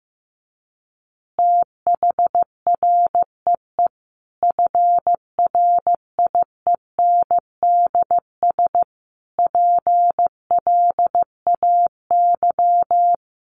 Morse code, heard as a keyed tone, about 15 wpm, 700 Hz. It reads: THREE FRIENDS PLAY
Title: Easy Phrases 15wpm